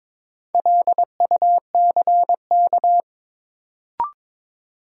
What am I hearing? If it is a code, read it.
Morse code, heard as a keyed tone, about 22 wpm, 700 Hz. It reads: LUCK
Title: Sentences from Top 1500 Words - New Word Drill - 22wpm